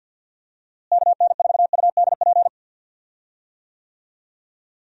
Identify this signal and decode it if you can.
Morse code, heard as a keyed tone, about 51 wpm, 700 Hz. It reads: KN4UDP
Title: USA Call Signs - 6 Characters - Repeated 3x - 50wpm